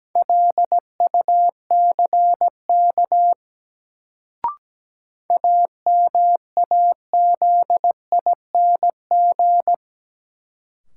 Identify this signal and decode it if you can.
Morse code, heard as a keyed tone, about 17 wpm, 700 Hz. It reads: LUCK AMAZING
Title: Sentences from Top 1500 Words - New Word Drill - 17wpm